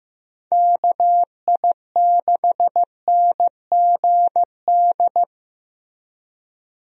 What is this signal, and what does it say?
Morse code, heard as a keyed tone, about 15 wpm, 700 Hz. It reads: KI6NGD
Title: USA Call Signs - 6 Characters - Repeated 3x - 15wpm